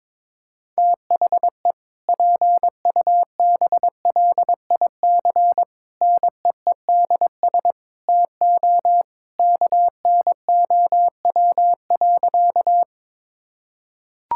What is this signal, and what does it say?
Morse code, heard as a keyed tone, about 21 wpm, 700 Hz. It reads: THE PUBLIC NEEDS TO KNOW.
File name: practice / 22wpm / Sentences with 2-Word Phrases - 22wpm